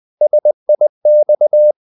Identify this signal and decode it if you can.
Morse code, heard as a keyed tone, about 20 wpm, 600 Hz. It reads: SIX
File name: the old man and the sea / the old man and the sea part 1 20wpm